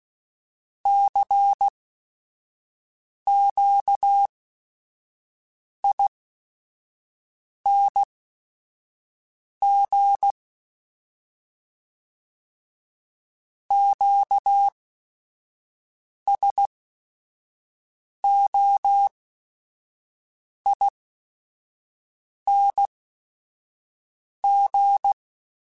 Morse code, heard as a keyed tone, about 16 wpm, spaced as for 5 wpm, 800 Hz. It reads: CQING QSOING
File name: Special INGs / Special INGs-mp3s / Sngl Ing @5